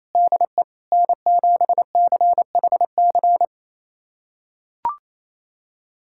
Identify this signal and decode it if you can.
Morse code, heard as a keyed tone, about 28 wpm, 700 Hz. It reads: DE N7CHC
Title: USA Callsign DE Callsign - 5 Characters - 28wpm